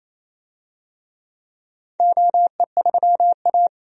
Morse code, heard as a keyed tone, about 28 wpm, 700 Hz. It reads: OE3A
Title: DXCC-100 Call Signs - 4 Characters - 28wpm